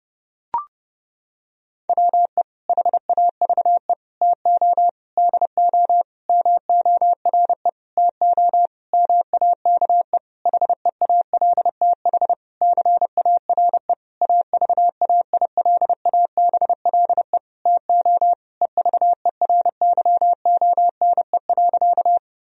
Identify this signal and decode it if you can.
Morse code, heard as a keyed tone, about 30 wpm, 700 Hz. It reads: WE HAVE TO DO MORE TO MAKE HEALTH CARE AVAILABLE TO EVERYONE.
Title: Sentences from Top 1000 Words - Review Part 3 - 30wpm